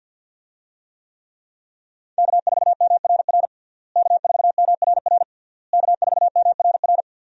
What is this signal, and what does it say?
Morse code, heard as a keyed tone, about 51 wpm, 700 Hz. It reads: K4MRR K4MRR K4MRR
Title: USA Call Signs - 5 Characters - Repeated 3x - 50wpm